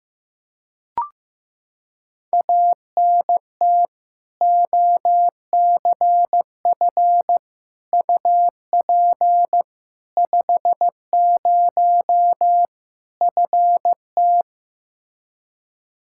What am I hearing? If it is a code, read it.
Morse code, heard as a keyed tone, about 15 wpm, 700 Hz. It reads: ANT OCF UP 50 FT
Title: K1USN SST QSO Practice 15wpm